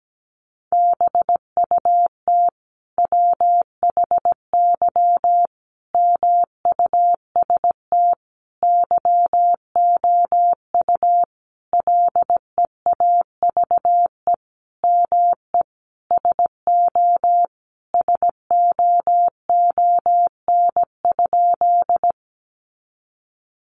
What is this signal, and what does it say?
Morse code, heard as a keyed tone, about 17 wpm, 700 Hz. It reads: BUT WHY MUST YOU LEAVE ME SO SOON?
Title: Sentences from Top 400 Words 17wpm